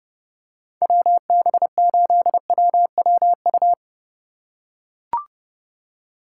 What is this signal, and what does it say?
Morse code, heard as a keyed tone, about 29 wpm, 700 Hz. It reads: WB8WWU